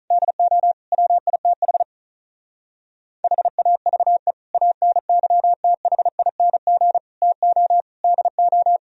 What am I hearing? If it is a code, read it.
Morse code, heard as a keyed tone, about 44 wpm, 700 Hz. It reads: DO WITH HAVE ANYTHING TO DO
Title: Speed-Racing - Sets of 5 Words 35wpm